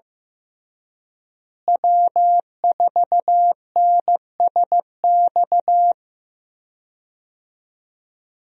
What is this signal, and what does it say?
Morse code, heard as a keyed tone, about 15 wpm, 700 Hz. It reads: W4NSX
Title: USA Call Signs - 5 Characters - Repeated 3x - 15wpm